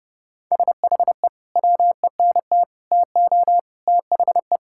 Morse code, heard as a keyed tone, about 30 wpm, 700 Hz. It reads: SHE WENT TO THE
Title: Sentences from Top 1600 Words 30wpm